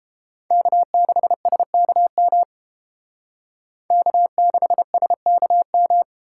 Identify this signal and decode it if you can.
Morse code, heard as a keyed tone, about 28 wpm, 700 Hz. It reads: K6SKM K6SKM